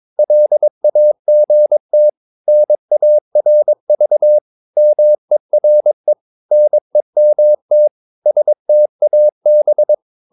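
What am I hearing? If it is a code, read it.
Morse code, heard as a keyed tone, about 22 wpm, 600 Hz. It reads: LAGT NARV MERE NEMT STAB
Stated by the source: Morse practice of short Danish words at 22 wpm, from CW med Gnister.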